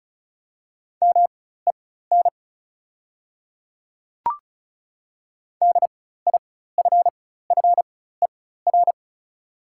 Morse code, heard as a keyed tone, about 36 wpm, spaced as for 16 wpm, 690 Hz. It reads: MEN DIFFER
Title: 4x Character Spacing - Top 200 Words - 35wpm